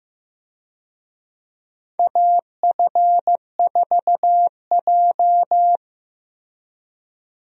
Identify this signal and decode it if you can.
Morse code, heard as a keyed tone, about 15 wpm, 700 Hz. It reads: AF4J
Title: USA Call Signs - 4 Characters - Repeated 2x - 15wpm